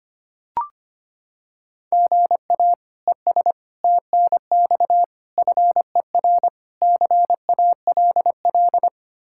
Morse code, heard as a keyed tone, about 25 wpm, 700 Hz. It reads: GA ES TNX FER CALL
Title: Practice QSO - Exchange 1 - Beginning - 25wpm